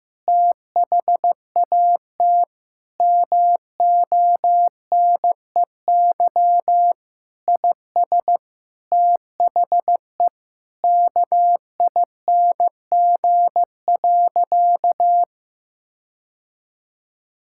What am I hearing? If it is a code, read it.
Morse code, heard as a keyed tone, about 15 wpm, 700 Hz. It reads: THAT MONEY IS THE KING.
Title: Sentences from Top 1600 Words 15wpm